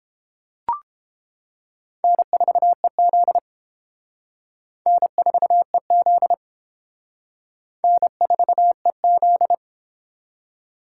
Morse code, heard as a keyed tone, about 30 wpm, 700 Hz. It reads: N4EZ N4EZ N4EZ